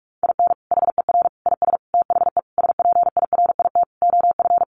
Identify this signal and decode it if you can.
Morse code, heard as a keyed tone, about 44 wpm, 700 Hz. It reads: IN HER IS THE SPIRIT OF
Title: Sentences from Top 1400 Words 45wpm